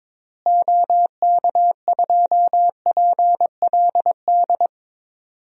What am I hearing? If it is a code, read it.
Morse code, heard as a keyed tone, about 21 wpm, 700 Hz. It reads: OK2PLD
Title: DXCC-100 Call Signs - 6 Characters - Repeated 3x - 22wpm